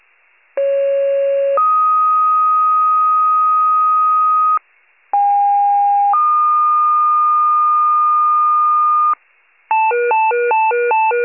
Two-tone page, heard as a one-second tone, then a three-second tone, about 550 then 1250 Hz, about 810 then 1220 Hz. It is heard from 0.6 to 9.1 s.
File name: wcalert